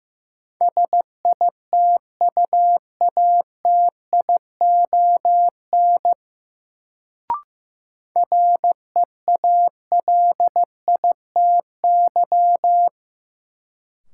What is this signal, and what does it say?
Morse code, heard as a keyed tone, about 15 wpm, 700 Hz. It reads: SITUATION REALITY
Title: Sentences from Top 1200 Words - New Word Drill - 15wpm